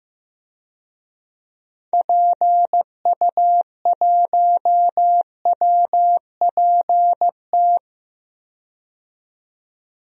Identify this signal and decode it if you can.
Morse code, heard as a keyed tone, about 16 wpm, 700 Hz. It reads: PU1WPT
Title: DXCC-100 Call Signs - 6 Characters - 15wpm